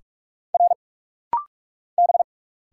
Morse code, heard as a keyed tone, about 45 wpm, 690 Hz. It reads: R B
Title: Single Letters - Rapid-Fire 45wpm